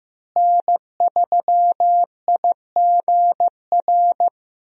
Morse code, heard as a keyed tone, about 15 wpm, 700 Hz. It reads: N3IGR
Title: USA Call Signs - 5 Characters - Repeated 3x - 15wpm